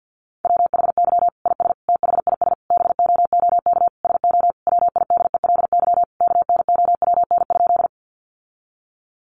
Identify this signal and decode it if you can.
Morse code, heard as a keyed tone, about 50 wpm, 700 Hz. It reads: WHY IS THIS BOOK SO WIDELY KNOWN?
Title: Sentences from Top 1200 Words 50wpm